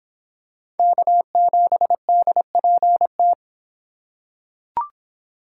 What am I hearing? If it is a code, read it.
Morse code, heard as a keyed tone, about 25 wpm, 700 Hz. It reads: K7DPT